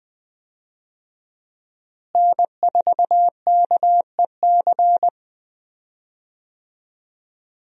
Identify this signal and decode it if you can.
Morse code, heard as a keyed tone, about 20 wpm, 700 Hz. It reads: N4KEC
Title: USA Call Signs - 5 Characters - Repeated 3x - 20wpm